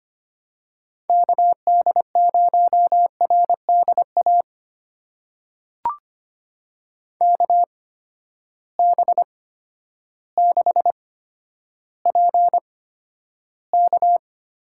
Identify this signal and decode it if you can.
Morse code, heard as a keyed tone, about 25 wpm, spaced as for 11 wpm, 700 Hz. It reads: KD0RDA KB6PK
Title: USA Call Signs - 8x Character Spacing - 25wpm